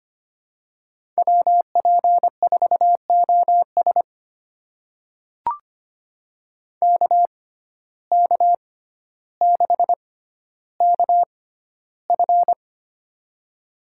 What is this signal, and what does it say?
Morse code, heard as a keyed tone, about 25 wpm, spaced as for 13 wpm, 700 Hz. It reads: WP4OS KK6KF
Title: USA Call Signs - 5 Characters - 6x Character Spacing - 25wpm